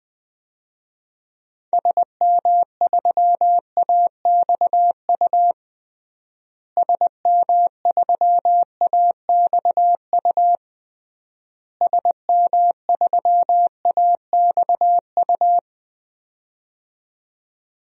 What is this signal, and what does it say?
Morse code, heard as a keyed tone, about 20 wpm, 700 Hz. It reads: SM3AXU SM3AXU SM3AXU
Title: DXCC-100 Call Signs - 6 Characters - Repeated 3x - 20wpm